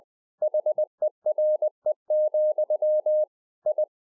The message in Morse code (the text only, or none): HERE, I